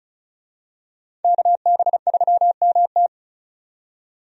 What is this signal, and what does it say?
Morse code, heard as a keyed tone, about 36 wpm, 700 Hz. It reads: KB3MT
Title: USA Call Signs - 5 Characters - 35wpm